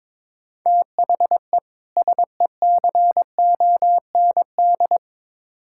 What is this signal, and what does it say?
Morse code, heard as a keyed tone, about 22 wpm, 700 Hz. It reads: THE SECOND
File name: Build A Phrase - 2-Word Phrases - 22wpm